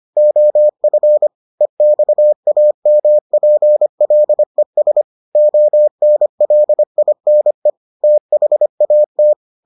Morse code, heard as a keyed tone, about 25 wpm, 600 Hz. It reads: OF EXAMPLES ONLINE THAT